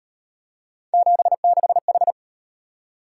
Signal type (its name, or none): Morse code